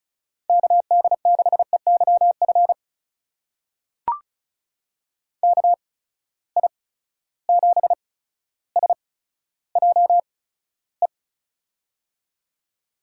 Morse code, heard as a keyed tone, about 35 wpm, spaced as for 15 wpm, 700 Hz. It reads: KD6EYF KI7SJE
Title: USA Call Signs - 6 Characters - 8x Character Spacing - 35wpm